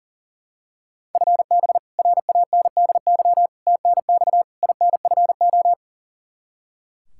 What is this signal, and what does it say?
Morse code, heard as a keyed tone, about 40 wpm, 700 Hz. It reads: FB RANDY TNX INFO
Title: Practice QSO - Exchange 3 - More Info - 40wpm